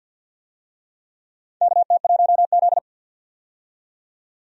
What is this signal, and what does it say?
Morse code, heard as a keyed tone, about 52 wpm, 700 Hz. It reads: KT1Z
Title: USA Call Signs - 4 Characters - 50wpm